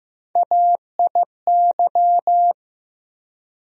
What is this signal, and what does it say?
Morse code, heard as a keyed tone, about 15 wpm, 700 Hz. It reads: AIY
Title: USA Call Sign Postfixes - 3 Letters 15wpm